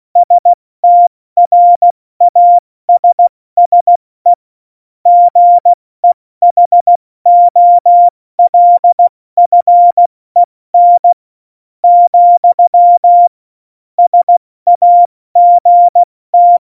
Morse code, heard as a keyed tone, about 16 wpm, spaced as for 14 wpm, 700 Hz. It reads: STRASSE GEHOLFEN, SAGT
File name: Morsekurs / Witze / Witz 30. April 2019, 70 F80